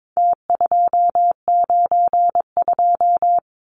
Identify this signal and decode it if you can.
Morse code, heard as a keyed tone, about 22 wpm, 700 Hz. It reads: T292
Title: Numbers - Contest Style - 4-Digit 22wpm